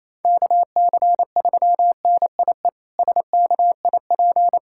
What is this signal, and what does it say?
Morse code, heard as a keyed tone, about 28 wpm, 700 Hz. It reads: KC3NIE SKIP
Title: K1USN SST QSO Practice - 28wpm